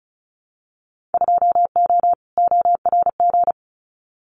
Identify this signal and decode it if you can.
Morse code, heard as a keyed tone, about 34 wpm, 700 Hz. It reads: 2O ORG
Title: Winter Field Day Exchanges 35wpm